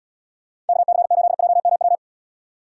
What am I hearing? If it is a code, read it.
Morse code, heard as a keyed tone, about 76 wpm, 700 Hz. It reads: KC8JMW